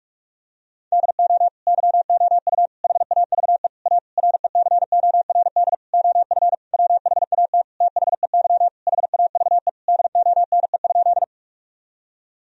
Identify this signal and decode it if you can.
Morse code, heard as a keyed tone, about 46 wpm, 700 Hz. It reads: DO YOU HAVE A RECORD OF WHAT THEY HAVE DONE?
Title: Sentences from Top 1000 Words - Review Part 3 - 45wpm